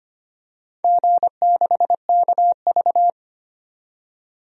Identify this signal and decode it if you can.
Morse code, heard as a keyed tone, about 25 wpm, 700 Hz. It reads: G6KV
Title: DXCC-100 Call Signs - 4 Characters - 25wpm